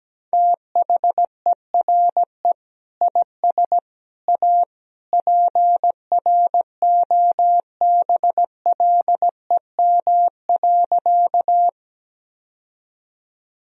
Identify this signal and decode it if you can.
Morse code, heard as a keyed tone, about 17 wpm, 700 Hz. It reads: THERE IS A PROBLEM.
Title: Sentences from Top 1000 Words - Review Part 2 - 17wpm